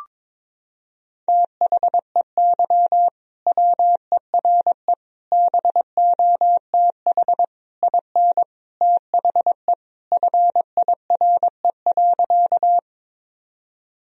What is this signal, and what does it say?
Morse code, heard as a keyed tone, about 22 wpm, 700 Hz. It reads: THEY WERE BOTH IN THE FIRE.
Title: Sentences from Top 1000 Words - Review Part 1 - 22wpm